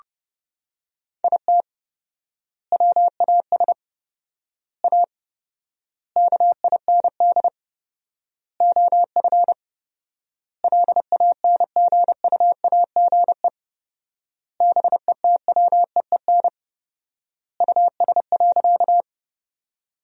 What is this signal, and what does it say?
Morse code, heard as a keyed tone, about 32 wpm, 700 Hz. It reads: IT WAS A KIND OF LANGUAGE BETWEEN US.
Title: 4x Word-Spacing - Sentences from Top 500 Words - 30wpm